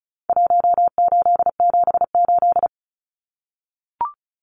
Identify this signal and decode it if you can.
Morse code, heard as a keyed tone, about 34 wpm, 700 Hz. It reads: 1878
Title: Numbers - Contest Style - 4-Digit 35wpm